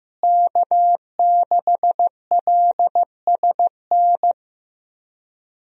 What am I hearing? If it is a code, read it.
Morse code, heard as a keyed tone, about 15 wpm, 700 Hz. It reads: K6LSN
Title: USA Call Signs - 5 Characters - Repeated 2x - 15wpm